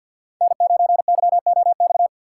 Morse code, heard as a keyed tone, about 53 wpm, 700 Hz. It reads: N9YOX
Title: USA Call Signs - 5 Characters - 6x Character Spacing - 50wpm